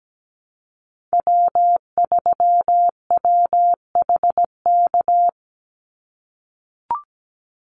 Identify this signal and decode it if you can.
Morse code, heard as a keyed tone, about 17 wpm, 700 Hz. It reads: W3WHK
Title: Call Signs - 17wpm